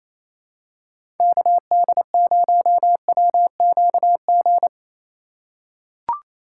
Morse code, heard as a keyed tone, about 28 wpm, 700 Hz. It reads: KD0WQG
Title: USA Call Signs - 3x Character Spacing - 28wpm